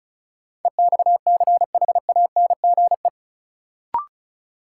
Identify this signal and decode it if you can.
Morse code, heard as a keyed tone, about 36 wpm, 700 Hz. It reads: EXCHANGE
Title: Sentences from Top 1000 Words - Review Part 6 - New Word Drill - 35wpm